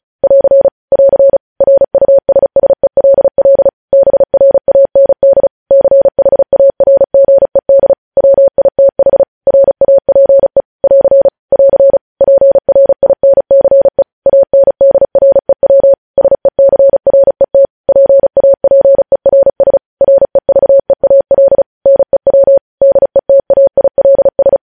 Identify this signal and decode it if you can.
Morse code, heard as a keyed tone, about 36 wpm, 550 Hz. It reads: <AR> <AR> RUSSELL BRAND CHARGED WITH RAPE <AR> <AR> PRINCE ANDREW SECRET PAPERS REVEAL NEW DETAILS
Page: news headlines in morse code at 35 wpm